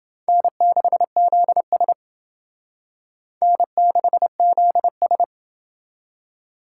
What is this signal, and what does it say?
Morse code, heard as a keyed tone, about 30 wpm, 700 Hz. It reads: N6ZS N6ZS